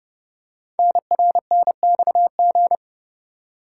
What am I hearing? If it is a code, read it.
Morse code, heard as a keyed tone, about 30 wpm, 700 Hz. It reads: NRNXG
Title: Code Groups - 5 Letters - 30wpm